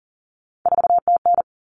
Morse code, heard as a keyed tone, about 40 wpm, 700 Hz. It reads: 4TN
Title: Common Cut Numbers - 3-Digit 40wpm